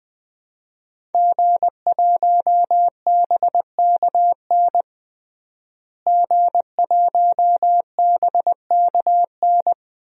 Morse code, heard as a keyed tone, about 20 wpm, 700 Hz. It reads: G1BKN G1BKN
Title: DXCC-100 Call Signs - 5 Characters - Repeated 3x - 20wpm